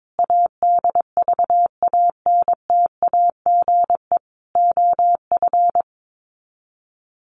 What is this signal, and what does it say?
Morse code, heard as a keyed tone, about 22 wpm, 700 Hz. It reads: ADVANTAGE OF
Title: Sets of 4 Words 22wpm